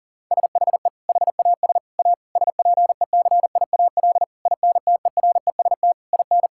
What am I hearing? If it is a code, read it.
Morse code, heard as a keyed tone, about 40 wpm, 700 Hz. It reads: SHE HAS A SPECIAL INTEREST IN
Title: Sentences from Top 1000 Words - Review Part 5 - 40wpm